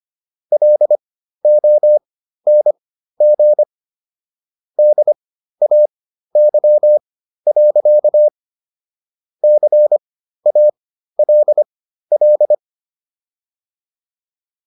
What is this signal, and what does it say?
Morse code, heard as a keyed tone, about 25 wpm, spaced as for 13 wpm, 600 Hz. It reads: LONG DAY. CALL